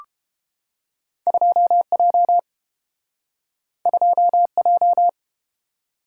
Morse code, heard as a keyed tone, about 29 wpm, 700 Hz. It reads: 2J 2J